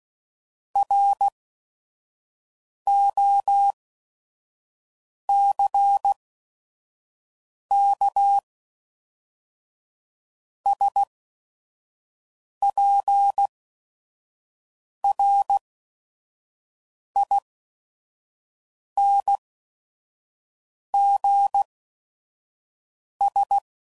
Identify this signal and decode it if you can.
Morse code, heard as a keyed tone, about 16 wpm, spaced as for 5 wpm, 800 Hz. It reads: ROCKSPRINGS
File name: CW Code Groups @5WPM / fromSM's / Cities